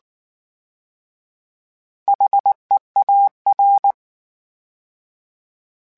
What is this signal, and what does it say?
Morse code, heard as a keyed tone, about 19 wpm, 800 Hz. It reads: HEAR
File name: H Prefixes / H mp3s